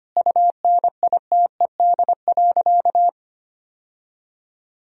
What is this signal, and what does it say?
Morse code, heard as a keyed tone, about 25 wpm, 700 Hz. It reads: UNITED.
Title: Sentences from Top 1000 Words - Review Part 4 - 25wpm